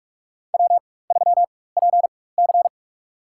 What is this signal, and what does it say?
Morse code, heard as a keyed tone, about 46 wpm, spaced as for 21 wpm, 700 Hz. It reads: W3PC
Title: USA Call Signs - 4 Characters - 4x Character Spacing - 45wpm